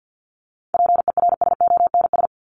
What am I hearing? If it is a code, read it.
Morse code, heard as a keyed tone, about 50 wpm, 700 Hz. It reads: PERSONS
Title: ES:S Suffix - Top 100 Words with Base Words 50wpm